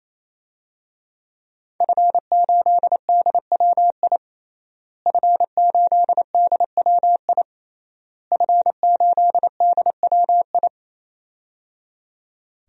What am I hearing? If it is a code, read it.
Morse code, heard as a keyed tone, about 28 wpm, 700 Hz. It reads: F8DWI F8DWI F8DWI
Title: DXCC-100 Call Signs - 5 Characters - Repeated 3x - 28wpm